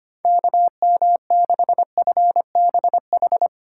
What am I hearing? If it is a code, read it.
Morse code, heard as a keyed tone, about 25 wpm, 700 Hz. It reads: KM6FBH
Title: USA Call Signs - 6 Characters - 3x Character Spacing - 25wpm